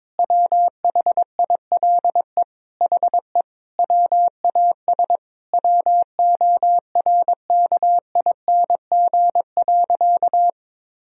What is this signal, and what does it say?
Morse code, heard as a keyed tone, about 23 wpm, 700 Hz. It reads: WHILE HE WAS WORKING.
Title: Sentences from Top 1000 Words - Review Part 4 - 22wpm